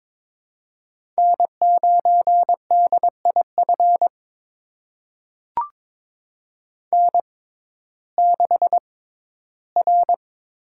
Morse code, heard as a keyed tone, about 22 wpm, spaced as for 14 wpm, 700 Hz. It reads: N9DIF N6R
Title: USA Call Signs - 6x Character Spacing - 22wpm